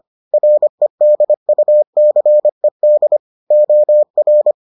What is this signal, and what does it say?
Morse code, heard as a keyed tone, about 25 wpm, 600 Hz. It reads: REDUCED OR